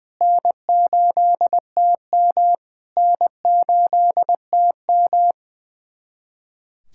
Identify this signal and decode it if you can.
Morse code, heard as a keyed tone, about 20 wpm, 690 Hz. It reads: N8TM N8TM
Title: SOTA QSO Practice 20wpm